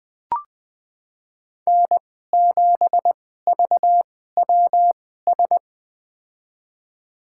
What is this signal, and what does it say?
Morse code, heard as a keyed tone, about 20 wpm, spaced as for 14 wpm, 700 Hz. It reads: N7VWS
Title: USA Call Signs - 5 Characters - 2x Character Spacing - 20wpm